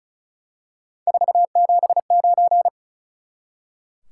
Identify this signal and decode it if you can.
Morse code, heard as a keyed tone, about 36 wpm, 700 Hz. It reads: 479
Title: Top 180 QSO Elements 35wpm